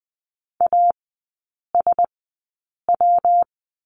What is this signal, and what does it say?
Morse code, heard as a keyed tone, about 19 wpm, 700 Hz. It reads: A S W
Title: 114 Lesson 19e - Letter w - Three Characters 20wpm